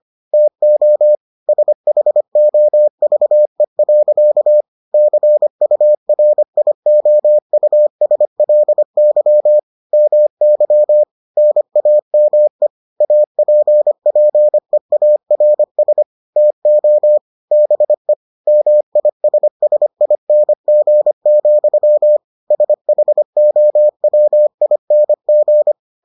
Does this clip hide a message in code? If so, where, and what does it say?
Yes: Morse code from 0.3 to 25.7 s, heard as a keyed tone, about 25 wpm, 600 Hz. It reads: TO SHOVE. CURIOUSLY MY NAME APPEARS TO BE MISSING, SHOWING